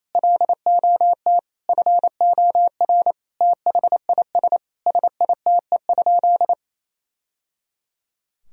Morse code, heard as a keyed tone, about 28 wpm, 700 Hz. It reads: LOT FOR THIS SITE?
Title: Sentences from Top 1000 Words - Review Part 4 - 28wpm